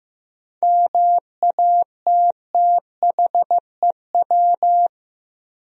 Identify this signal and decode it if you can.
Morse code, heard as a keyed tone, about 15 wpm, 700 Hz. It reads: MATTHEW
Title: USA Common Names - 15wpm